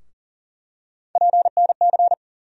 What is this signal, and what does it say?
Morse code, heard as a keyed tone, about 40 wpm, 700 Hz. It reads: PNC
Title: USA Call Sign Postfixes - 3 Letters 40wpm